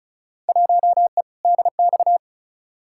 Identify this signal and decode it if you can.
Morse code, heard as a keyed tone, about 35 wpm, 700 Hz. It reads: 1E DX
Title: Field Day Exchanges Encore - 35wpm